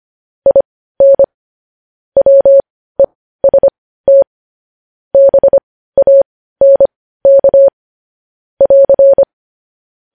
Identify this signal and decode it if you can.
Morse code, heard as a keyed tone, about 25 wpm, spaced as for 15 wpm, 550 Hz. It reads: IN WEST BANK <AR>
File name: podcast / News headlines-1x-25x15